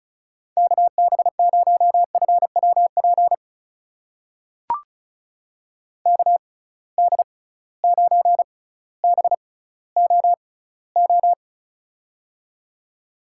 Morse code, heard as a keyed tone, about 36 wpm, spaced as for 18 wpm, 700 Hz. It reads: KB0FWP KD9BOO
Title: USA Call Signs - 6 Characters - 6x Character Spacing - 35wpm